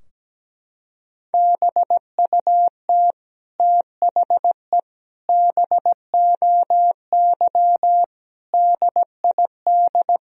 Morse code, heard as a keyed tone, about 17 wpm, 700 Hz. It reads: BUT THE BOY DID
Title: Sentences from Top 1000 Words - Review Part 3 - 17wpm